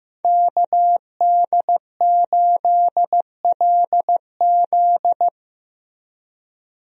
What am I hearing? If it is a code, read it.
Morse code, heard as a keyed tone, about 15 wpm, 700 Hz. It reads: KD8LZ
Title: USA Call Signs - 5 Characters - Repeated 3x - 15wpm